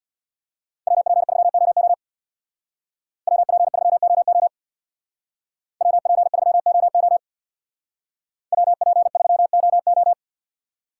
Morse code, heard as a keyed tone, about 59 wpm, 700 Hz. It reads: WP3OO WP3OO WP3OO WP3OO
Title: Speed Racing - USA 2x2 Call Signs 50wpm